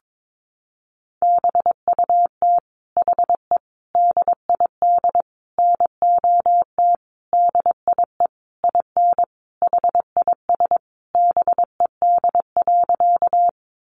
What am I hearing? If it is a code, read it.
Morse code, heard as a keyed tone, about 22 wpm, 700 Hz. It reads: BUT HE DID NOT DIE IN HIS BED.
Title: Sentences from Top 1300 Words 22wpm